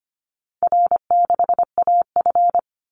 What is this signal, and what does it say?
Morse code, heard as a keyed tone, about 25 wpm, 700 Hz. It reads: R6AF
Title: Easy Call Signs Repeated 3x - 25wpm